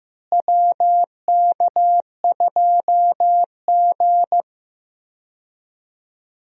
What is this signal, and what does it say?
Morse code, heard as a keyed tone, about 15 wpm, 700 Hz. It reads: WK2G
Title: USA Call Signs - 4 Characters - Repeated 3x - 15wpm